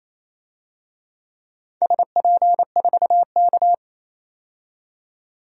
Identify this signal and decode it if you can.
Morse code, heard as a keyed tone, about 28 wpm, 700 Hz. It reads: SP4K
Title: DXCC-100 Call Signs - 4 Characters - 28wpm